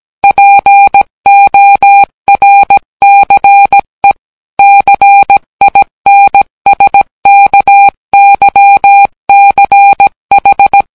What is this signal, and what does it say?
Morse code, heard as a keyed tone, about 17 wpm, 800 Hz. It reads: PORCE CINSKYCH